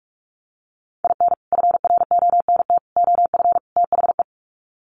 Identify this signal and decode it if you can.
Morse code, heard as a keyed tone, about 44 wpm, 700 Hz. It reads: IN FRONT OF THE
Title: Sets of 4 Words 45wpm